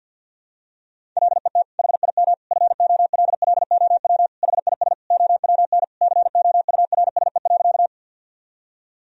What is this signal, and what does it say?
Morse code, heard as a keyed tone, about 52 wpm, 700 Hz. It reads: LET HIM FOLLOW HIS OWN COURSE.
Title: Sentences from Top 1000 Words - Review Part 1 - 50wpm